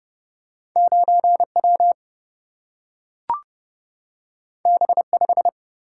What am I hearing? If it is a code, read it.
Morse code, heard as a keyed tone, about 29 wpm, 700 Hz. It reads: 9W B5
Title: Single-Letter Single-Number 30wpm